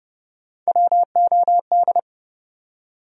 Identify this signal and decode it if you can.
Morse code, heard as a keyed tone, about 32 wpm, 700 Hz. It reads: WOD
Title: USA Call Sign Postfixes - 3 Letters 30wpm